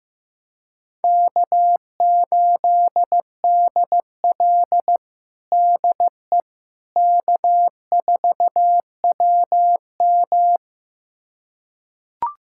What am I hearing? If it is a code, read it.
Morse code, heard as a keyed tone, about 16 wpm, 700 Hz. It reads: K8DL DE K4WM
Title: USA Callsign DE Callsign - 4 Characters - 15wpm